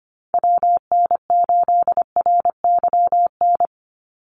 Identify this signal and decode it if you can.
Morse code, heard as a keyed tone, about 25 wpm, 700 Hz. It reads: WN8RYN
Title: USA Call Signs 25wpm